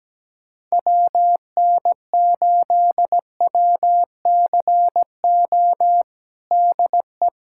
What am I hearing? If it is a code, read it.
Morse code, heard as a keyed tone, about 17 wpm, 700 Hz. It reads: WN8WCO DE
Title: USA Callsign DE Callsign - 6 Characters - 17wpm